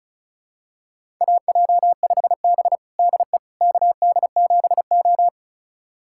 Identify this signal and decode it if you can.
Morse code, heard as a keyed tone, about 36 wpm, 700 Hz. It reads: AJ5B DE KD7O
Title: USA Callsign DE Callsign - 4 Characters - 35wpm